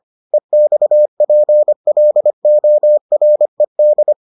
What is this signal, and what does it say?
Morse code, heard as a keyed tone, about 25 wpm, 600 Hz. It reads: EXPLORED